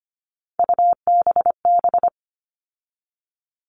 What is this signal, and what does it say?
Morse code, heard as a keyed tone, about 25 wpm, 700 Hz. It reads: UBB
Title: All Cut Numbers - 3-Digit 25wpm